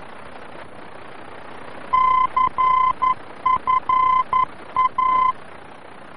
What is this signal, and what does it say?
Morse code, heard as a keyed tone, about 11 wpm, 1030 Hz. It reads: CFA
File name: france / mx / CFA-VOR-F2